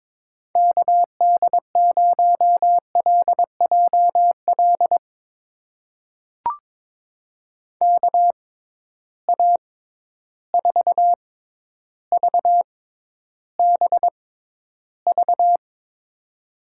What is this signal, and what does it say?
Morse code, heard as a keyed tone, about 22 wpm, spaced as for 11 wpm, 700 Hz. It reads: KD0LJL KA4VBV
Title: USA Call Signs - 6 Characters - 6x Character Spacing - 22wpm